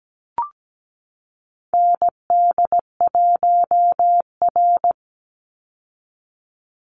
Morse code, heard as a keyed tone, about 17 wpm, 700 Hz. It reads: ND1R
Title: Call Signs Repeated 2x - 17wpm